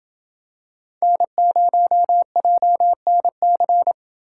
Morse code, heard as a keyed tone, about 28 wpm, 700 Hz. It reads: N0JNC